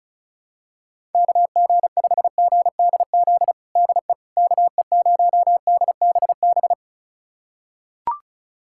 Morse code, heard as a keyed tone, about 34 wpm, 700 Hz. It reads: KG5GDZ DE KE0DBB
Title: USA Callsign DE Callsign - 6 Characters - 35wpm